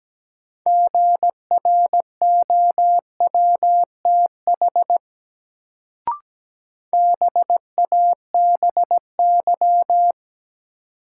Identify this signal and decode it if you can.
Morse code, heard as a keyed tone, about 17 wpm, 700 Hz. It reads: GROWTH BABY
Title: Sentences from Top 1300 Words - New Word Drill Alt - 17wpm